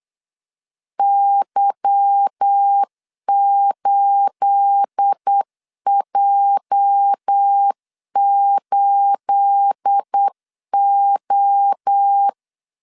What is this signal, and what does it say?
Morse code, heard as a keyed tone, about 8 wpm, 800 Hz. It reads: Y8J8O
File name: LEZIONI / 9 28 8wpm 3512